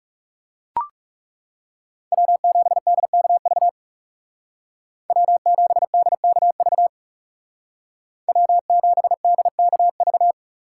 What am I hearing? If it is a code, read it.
Morse code, heard as a keyed tone, about 39 wpm, 700 Hz. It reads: W7DKV W7DKV W7DKV